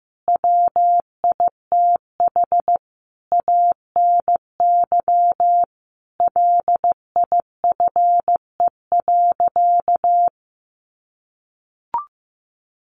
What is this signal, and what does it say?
Morse code, heard as a keyed tone, about 15 wpm, 700 Hz. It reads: WITH ANY LIFE.
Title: Sentences from Top 300 Words 15wpm